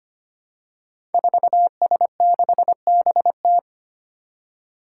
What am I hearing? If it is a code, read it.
Morse code, heard as a keyed tone, about 25 wpm, 700 Hz. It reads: 4S6BT
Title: Five Challenging Letter-Number Combinations 25wpm